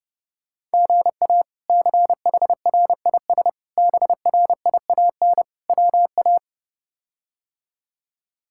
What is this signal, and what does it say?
Morse code, heard as a keyed tone, about 30 wpm, 700 Hz. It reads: GA CHRIS BRIAN WA
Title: K1USN SST QSO Practice - 30wpm